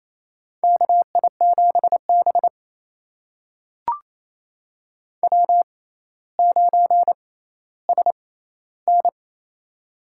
Morse code, heard as a keyed tone, about 28 wpm, spaced as for 14 wpm, 700 Hz. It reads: KI7B W9SN
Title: USA Call Signs - 4 Characters - 6x Character Spacing - 28wpm